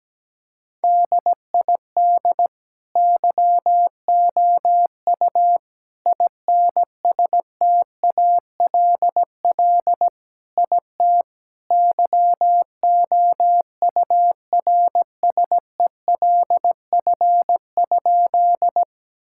Morse code, heard as a keyed tone, about 17 wpm, 700 Hz. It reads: DID YOU INSTALL IT YOURSELF?
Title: Sentences from Top 1600 Words 17wpm